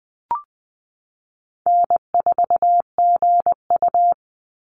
Morse code, heard as a keyed tone, about 20 wpm, 700 Hz. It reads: N4GU
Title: Call Signs - 20wpm